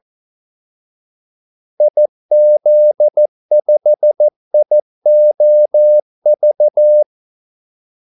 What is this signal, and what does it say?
Morse code, heard as a keyed tone, about 14 wpm, 600 Hz. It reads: IZ5IOV
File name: Callsigns-14wpm